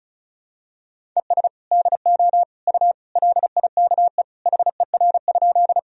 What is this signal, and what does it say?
Morse code, heard as a keyed tone, about 36 wpm, 700 Hz. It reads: ES DO U LIKE HER?
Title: Sentences from Top 100 Words and Abbreviations 35wpm